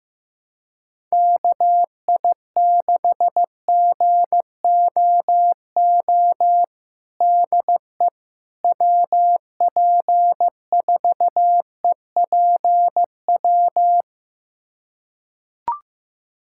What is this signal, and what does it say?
Morse code, heard as a keyed tone, about 15 wpm, 700 Hz. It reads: KI6GOO DE WP4EPW
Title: USA Callsign DE Callsign - 6 Characters - 15wpm